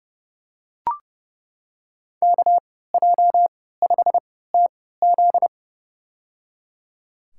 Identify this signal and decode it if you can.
Morse code, heard as a keyed tone, about 30 wpm, spaced as for 17 wpm, 700 Hz. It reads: KJ5TZ
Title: USA Call Signs - 5 Characters - 3x Character Spacing - 30wpm